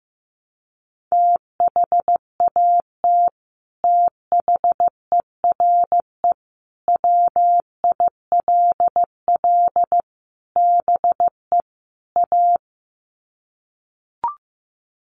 Morse code, heard as a keyed tone, about 15 wpm, 700 Hz. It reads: THAT THERE WILL BE A
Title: Sets of 5 Words Encore2 15wpm